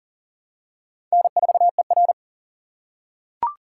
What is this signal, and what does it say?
Morse code, heard as a keyed tone, about 39 wpm, 690 Hz. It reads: N4ER
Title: USA Call Signs - 4 Characters - 2x Character Spacing - 40wpm